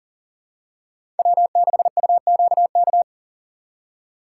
Morse code, heard as a keyed tone, about 40 wpm, 700 Hz. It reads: W6UQK
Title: USA Call Signs - 5 Characters - 4x Character Spacing - 40wpm